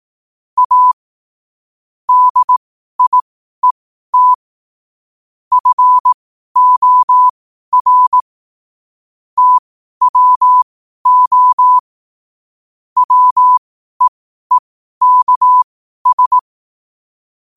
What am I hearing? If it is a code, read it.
Morse code, heard as a keyed tone, about 18 wpm, spaced as for 12 wpm, 1000 Hz. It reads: A DIET FOR TWO WEEKS
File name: qotdfiles / 12wpm